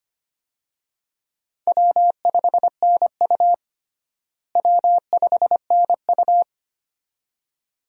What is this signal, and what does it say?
Morse code, heard as a keyed tone, about 25 wpm, 700 Hz. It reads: W5NU W5NU
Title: USA Call Signs - 4 Characters - Repeated 2x - 25wpm